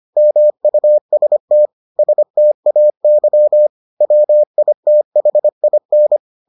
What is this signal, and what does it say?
Morse code, heard as a keyed tone, about 25 wpm, 600 Hz. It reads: MUST STAY WITHIN